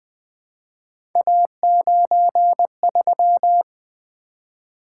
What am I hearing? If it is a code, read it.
Morse code, heard as a keyed tone, about 20 wpm, 700 Hz. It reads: A93
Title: Three Letter-Number Combinations - 20wpm